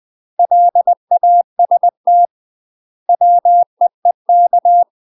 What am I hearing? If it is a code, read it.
Morse code, heard as a keyed tone, about 20 wpm, 700 Hz. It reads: LAST WEEK
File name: UK MASTER QSOs TRAINING / 20WPM / UK MASTER QSOs 0069